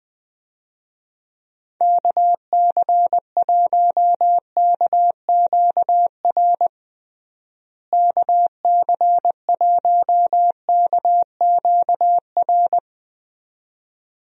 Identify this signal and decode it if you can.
Morse code, heard as a keyed tone, about 20 wpm, 700 Hz. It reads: KC1KQR KC1KQR
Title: USA Call Signs - 6 Characters - Repeated 2x - 20wpm